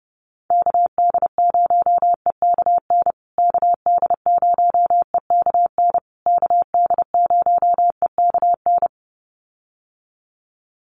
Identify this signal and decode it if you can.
Morse code, heard as a keyed tone, about 30 wpm, 700 Hz. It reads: KD0EKN KD0EKN KD0EKN
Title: USA Call Signs Repeated 3x 30wpm